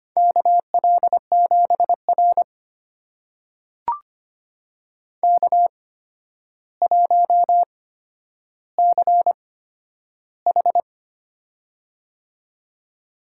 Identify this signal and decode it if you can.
Morse code, heard as a keyed tone, about 25 wpm, spaced as for 11 wpm, 700 Hz. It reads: KL7R K1CH
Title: USA Call Signs - 4 Characters - 8x Character Spacing - 25wpm